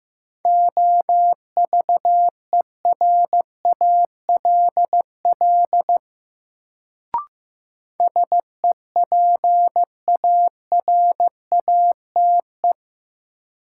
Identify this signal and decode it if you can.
Morse code, heard as a keyed tone, about 15 wpm, 700 Hz. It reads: OVERALL SEPARATE
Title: Sentences from Top 1000 Words - Review Part 7 - New Word Drill - 15wpm